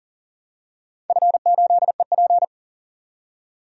Morse code, heard as a keyed tone, about 40 wpm, 700 Hz. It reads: F8EP
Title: DXCC-100 Call Signs - 4 Characters - 40wpm